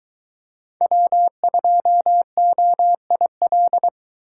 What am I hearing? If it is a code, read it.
Morse code, heard as a keyed tone, about 20 wpm, 700 Hz. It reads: W2OIL